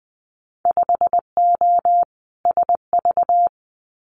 Morse code, heard as a keyed tone, about 20 wpm, 700 Hz. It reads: 5O SV
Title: Winter Field Day Exchanges Encore 20wpm